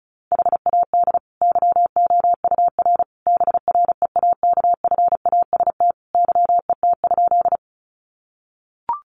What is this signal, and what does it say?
Morse code, heard as a keyed tone, about 34 wpm, 700 Hz. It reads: HAD YOUR BREAKFAST YET?
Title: Sentences from Top 1500 Words 35wpm